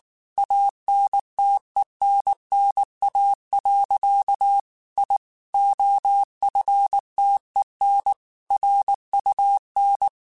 Morse code, heard as a keyed tone, about 19 wpm, 800 Hz. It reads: ANTENNA. I OFTEN RUN
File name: QSOs / QSOs-mp3s / kc7sac@18